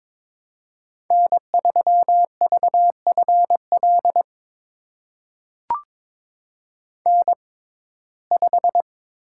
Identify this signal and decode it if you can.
Morse code, heard as a keyed tone, about 22 wpm, spaced as for 16 wpm, 700 Hz. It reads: N3VFL N5
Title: USA Call Signs - 5 Characters - 6x Character Spacing - 22wpm